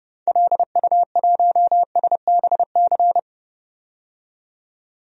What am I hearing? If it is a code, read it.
Morse code, heard as a keyed tone, about 30 wpm, 700 Hz. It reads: LU1SBC
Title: DXCC-100 Call Signs - 6 Characters - 30wpm